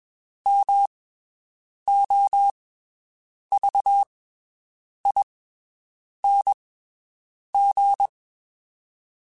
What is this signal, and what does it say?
Morse code, heard as a keyed tone, about 21 wpm, spaced as for 7 wpm, 800 Hz. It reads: MOVING
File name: (spaced) / QSOs Spaced-mp3s / WD0BGP 1 @20 sp